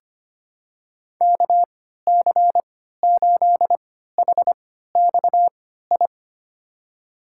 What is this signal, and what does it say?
Morse code, heard as a keyed tone, about 25 wpm, spaced as for 14 wpm, 700 Hz. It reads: KC8HXI
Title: USA Call Signs - 3x Character Spacing - 25wpm